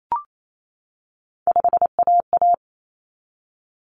Morse code, heard as a keyed tone, about 28 wpm, 700 Hz. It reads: 5AA
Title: Common Cut Numbers - 3-Digit 28wpm